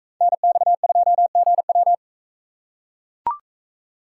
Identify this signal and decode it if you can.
Morse code, heard as a keyed tone, about 40 wpm, 700 Hz. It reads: NX2GW